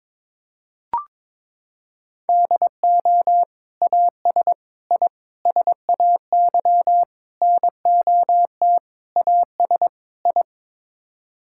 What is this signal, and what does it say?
Morse code, heard as a keyed tone, about 21 wpm, 700 Hz. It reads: DO AS I SAY NOT AS I
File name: practice / 22wpm / Sets of 7 Words - 22wpm